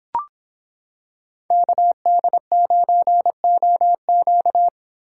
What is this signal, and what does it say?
Morse code, heard as a keyed tone, about 26 wpm, 700 Hz. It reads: KD9OQ